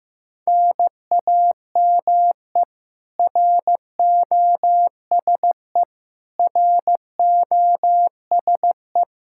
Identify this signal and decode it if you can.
Morse code, heard as a keyed tone, about 15 wpm, 700 Hz. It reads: NAME ROSE ROSE
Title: Practice QSO - Exchange 1 - Beginning - 15wpm